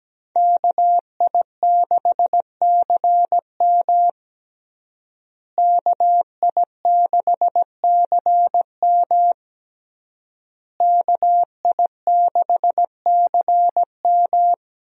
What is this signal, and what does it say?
Morse code, heard as a keyed tone, about 17 wpm, 700 Hz. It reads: KI6CM KI6CM KI6CM
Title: USA Call Signs - 5 Characters - Repeated 3x - 17wpm